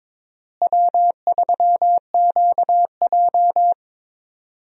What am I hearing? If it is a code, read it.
Morse code, heard as a keyed tone, about 22 wpm, 700 Hz. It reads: W3QJ
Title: USA Call Signs - 4 Characters - Repeated 3x - 22wpm